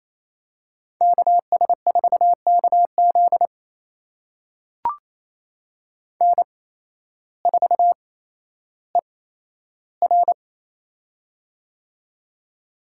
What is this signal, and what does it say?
Morse code, heard as a keyed tone, about 28 wpm, spaced as for 13 wpm, 700 Hz. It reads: KS4KZ N4ER
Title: USA Call Signs - 8x Character Spacing - 28wpm